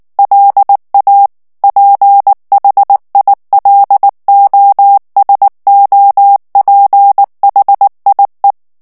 Morse code, heard as a keyed tone, about 19 wpm, 800 Hz. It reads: LA PHILOSOPHIE
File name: CW / FICHIERS-MP3 / FRANCAIS TEXTE CLAIR 2 VITESSE 18 MOTS MINUTES